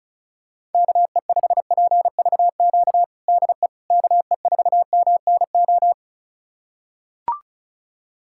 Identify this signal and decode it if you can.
Morse code, heard as a keyed tone, about 34 wpm, 700 Hz. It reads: KE5PVQ DE KE4MNO
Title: USA Callsign DE Callsign 35wpm